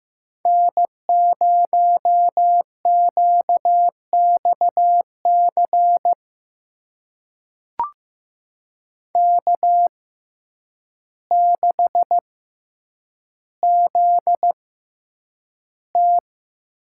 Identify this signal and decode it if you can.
Morse code, heard as a keyed tone, about 15 wpm, spaced as for 8 wpm, 700 Hz. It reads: N0QXC K6ZT
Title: USA Call Signs - 5 Characters - 6x Character Spacing - 15wpm